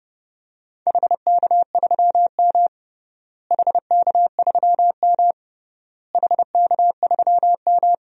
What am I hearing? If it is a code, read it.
Morse code, heard as a keyed tone, about 29 wpm, 700 Hz. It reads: HK3M HK3M HK3M
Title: DXCC-100 Call Signs - 4 Characters - Repeated 3x - 30wpm